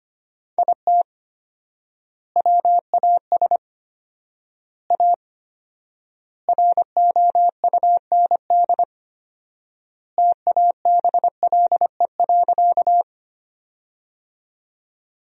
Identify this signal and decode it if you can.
Morse code, heard as a keyed tone, about 25 wpm, 700 Hz. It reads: IT WAS A ROUND TABLE.
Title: 4x Word-Spacing - Sentences from Top 400 Words - 25wpm